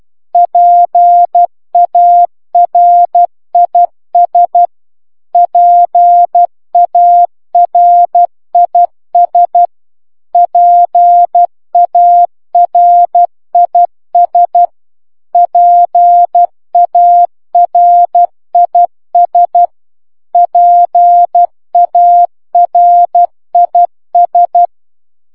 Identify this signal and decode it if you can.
Morse code, heard as a keyed tone, about 12 wpm, 700 Hz. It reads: PARIS PARIS PARIS PARIS PARIS